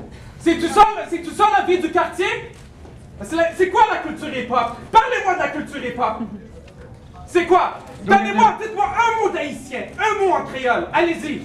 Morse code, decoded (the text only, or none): none